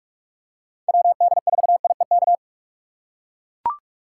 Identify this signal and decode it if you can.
Morse code, heard as a keyed tone, about 42 wpm, 700 Hz. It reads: WD4IEK
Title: USA Call Signs - 6x Character Spacing - 45wpm